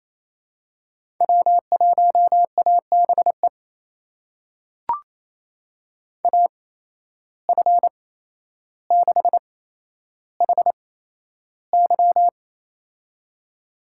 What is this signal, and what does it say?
Morse code, heard as a keyed tone, about 28 wpm, spaced as for 12 wpm, 700 Hz. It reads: W1ABE AF6HY
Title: USA Call Signs - 5 Characters - 8x Character Spacing - 28wpm